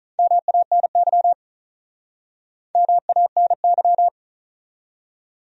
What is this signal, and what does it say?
Morse code, heard as a keyed tone, about 43 wpm, 700 Hz. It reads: MANY MANY
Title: Speed-Racing Top 100 Words 35wpm